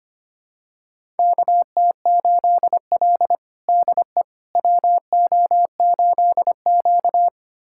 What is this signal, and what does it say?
Morse code, heard as a keyed tone, about 25 wpm, 700 Hz. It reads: KT8L DE WO8Q
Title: USA Callsign DE Callsign - 4 Characters - 25wpm